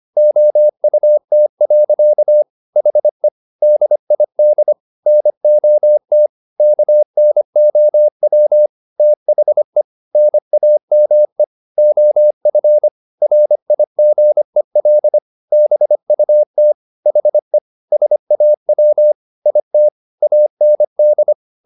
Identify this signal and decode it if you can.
Morse code, heard as a keyed tone, about 25 wpm, 600 Hz. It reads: OUT. HE DID NOT KNOW THE NAME OF RIGEL BUT HE SAW IT AND